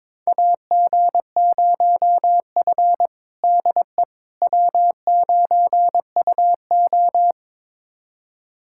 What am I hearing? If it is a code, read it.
Morse code, heard as a keyed tone, about 22 wpm, 700 Hz. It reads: AG0F DE W9UO
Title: USA Callsign DE Callsign - 4 Characters - 22wpm